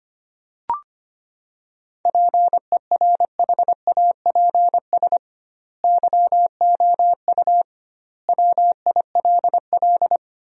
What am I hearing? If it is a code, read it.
Morse code, heard as a keyed tone, about 25 wpm, 700 Hz. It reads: PERHAPS YOU WILL
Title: 2x Word-Spacing - Sentences from Top 500 Words - 25wpm